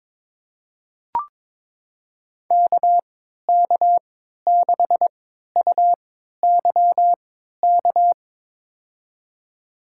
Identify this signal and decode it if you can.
Morse code, heard as a keyed tone, about 22 wpm, spaced as for 13 wpm, 700 Hz. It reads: KK6UYK
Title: USA Call Signs - 6 Characters - 3x Character Spacing - 22wpm